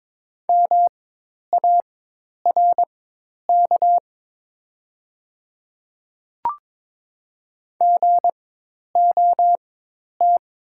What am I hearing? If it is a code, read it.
Morse code, heard as a keyed tone, about 22 wpm, spaced as for 10 wpm, 700 Hz. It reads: MARK GOT
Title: 4x Character Spacing - Top 300 Words - 22wpm